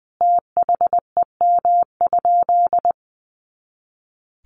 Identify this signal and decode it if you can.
Morse code, heard as a keyed tone, about 19 wpm, 700 Hz. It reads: THEM?
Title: Sentences from Top 200 Words 20wpm